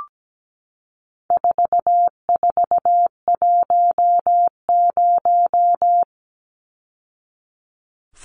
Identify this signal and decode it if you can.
Morse code, heard as a keyed tone, about 17 wpm, 700 Hz. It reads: 4410
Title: Numbers - 4-Digit 17wpm